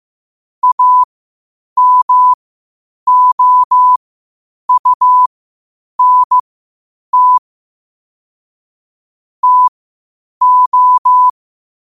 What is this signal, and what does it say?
Morse code, heard as a keyed tone, about 15 wpm, spaced as for 8 wpm, 1000 Hz. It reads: AMOUNT TO
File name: qotdfiles / 8wpm